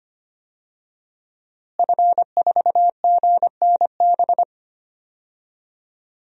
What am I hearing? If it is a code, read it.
Morse code, heard as a keyed tone, about 25 wpm, 700 Hz. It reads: F4GNB
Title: DXCC-100 Call Signs - 5 Characters - Repeated 2x - 25wpm